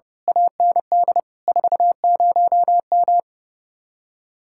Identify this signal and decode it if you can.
Morse code, heard as a keyed tone, about 28 wpm, 700 Hz. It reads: AND 40M
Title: Speed-Racing Common QSO Phrases - 30wpm